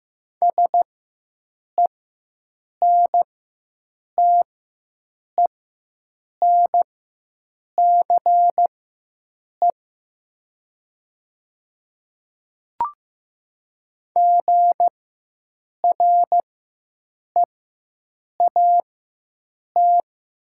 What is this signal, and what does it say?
Morse code, heard as a keyed tone, about 15 wpm, spaced as for 7 wpm, 700 Hz. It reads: SENTENCE GREAT
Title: 4x Character Spacing - Top 200 Words - 15wpm